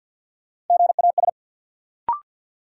Morse code, heard as a keyed tone, about 53 wpm, 700 Hz. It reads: GAS
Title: Sentences from Top 1100 Words - New Word Drill - 50wpm